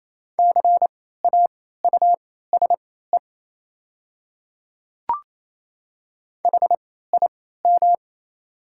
Morse code, heard as a keyed tone, about 27 wpm, spaced as for 16 wpm, 690 Hz. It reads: CAUSE HIM
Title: 3x Character Spacing - Top 200 Words - 28wpm